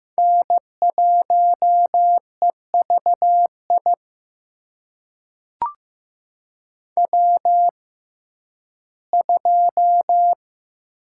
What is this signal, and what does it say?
Morse code, heard as a keyed tone, about 15 wpm, spaced as for 11 wpm, 700 Hz. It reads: N1EVI W2
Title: USA Call Signs - 5 Characters - 6x Character Spacing - 15wpm